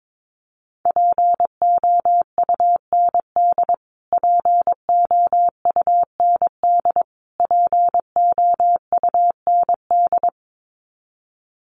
Morse code, heard as a keyed tone, about 22 wpm, 700 Hz. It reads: POUND POUND POUND
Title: Top 500 Words Repeated 22wpm